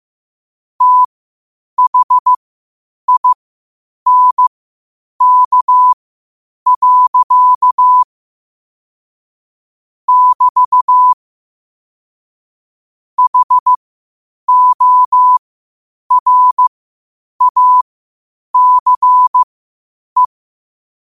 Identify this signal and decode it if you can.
Morse code, heard as a keyed tone, about 15 wpm, spaced as for 8 wpm, 1000 Hz. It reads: THINK. <BT> HORACE
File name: qotdfiles / 8wpm